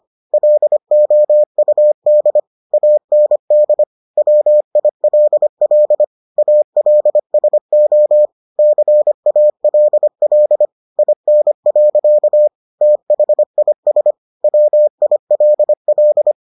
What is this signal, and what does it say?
Morse code, heard as a keyed tone, about 25 wpm, 600 Hz. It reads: LOUD AND WILL ALSO CALL IN. THIS WILL